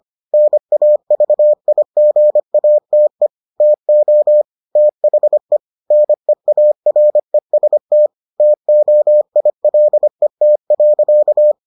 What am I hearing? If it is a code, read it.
Morse code, heard as a keyed tone, about 25 wpm, 600 Hz. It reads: NAVIGATE TO THE NEAREST TOILET.